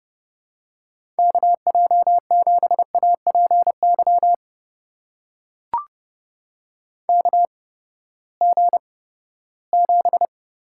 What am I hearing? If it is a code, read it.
Morse code, heard as a keyed tone, about 30 wpm, spaced as for 17 wpm, 700 Hz. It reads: KJ7APY KG7
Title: USA Call Signs - 8x Character Spacing - 30wpm